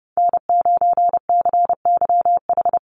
Morse code, heard as a keyed tone, about 30 wpm, 700 Hz. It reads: N9CYH
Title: USA Call Signs Repeated 3x 30wpm